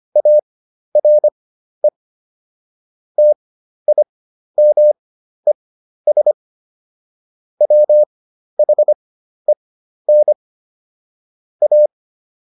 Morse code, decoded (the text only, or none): ARE TIMES WHEN A